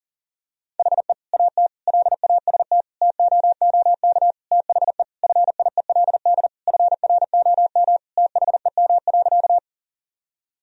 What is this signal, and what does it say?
Morse code, heard as a keyed tone, about 40 wpm, 700 Hz. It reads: HE AT LAST TOOK THE FIELD FROM THEM.
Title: Sentences from Top 1000 Words - Review Part 2 - 40wpm